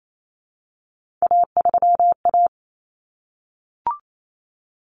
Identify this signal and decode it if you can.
Morse code, heard as a keyed tone, about 28 wpm, 700 Hz. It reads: A3A
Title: Common Cut Numbers - 3-Digit 28wpm